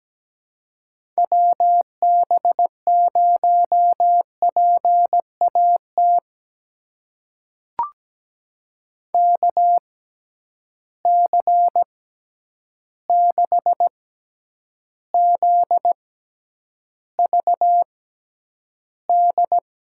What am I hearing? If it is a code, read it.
Morse code, heard as a keyed tone, about 17 wpm, spaced as for 9 wpm, 700 Hz. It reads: WB0PAT KC6ZVD
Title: USA Call Signs - 6 Characters - 6x Character Spacing - 17wpm